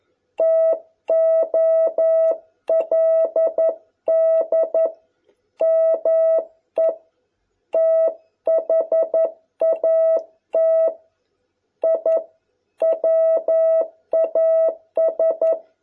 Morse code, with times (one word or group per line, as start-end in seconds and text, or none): 0.4-4.9 TOLD
5.6-6.9 ME
7.7-10.9 THAT
11.8-12.2 I
12.8-15.5 WAS